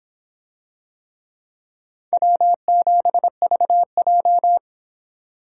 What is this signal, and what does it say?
Morse code, heard as a keyed tone, about 24 wpm, 700 Hz. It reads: W7VJ